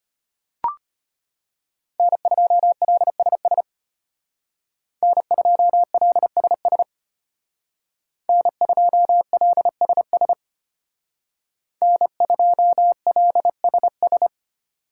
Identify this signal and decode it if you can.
Morse code, heard as a keyed tone, about 32 wpm, 700 Hz. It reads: N2LSS N2LSS N2LSS N2LSS